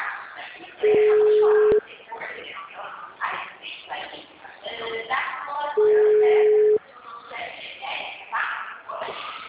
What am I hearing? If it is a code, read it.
Ringback tone (Europe, 425 Hz), with speech.
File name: out- -2017- - - .2549